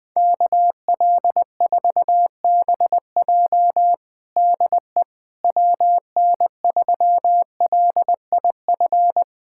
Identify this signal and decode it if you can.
Morse code, heard as a keyed tone, about 20 wpm, 700 Hz. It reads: KL4BJ DE WN3LIF
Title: USA Callsign DE Callsign 20wpm